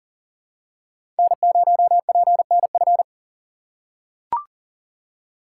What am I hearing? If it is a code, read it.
Morse code, heard as a keyed tone, about 40 wpm, 700 Hz. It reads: N0PNF
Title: USA Call Signs - 5 Characters - 3x Character Spacing - 40wpm